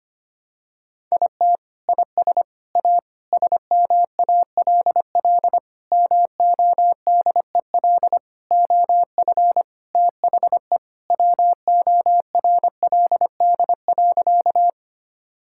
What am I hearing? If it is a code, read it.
Morse code, heard as a keyed tone, about 25 wpm, 700 Hz. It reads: IT IS A SMALL MODEL OF THE WORLD.
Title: Sentences from Top 1000 Words - Review Part 5 - 25wpm